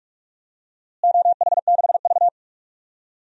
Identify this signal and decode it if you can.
Morse code, heard as a keyed tone, about 46 wpm, 700 Hz. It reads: OH6V
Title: DXCC-100 Call Signs - 4 Characters - Repeated 2x - 45wpm